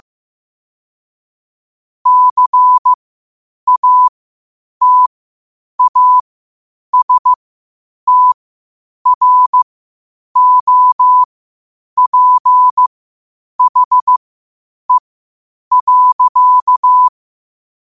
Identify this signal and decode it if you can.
Morse code, heard as a keyed tone, about 15 wpm, spaced as for 8 wpm, 1000 Hz. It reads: CATASTROPHE.